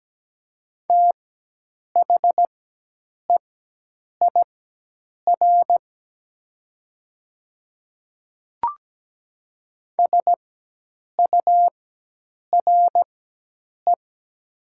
Morse code, heard as a keyed tone, about 17 wpm, spaced as for 8 wpm, 700 Hz. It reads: THEIR SURE
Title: 4x Character Spacing - Top 300 Words - 17wpm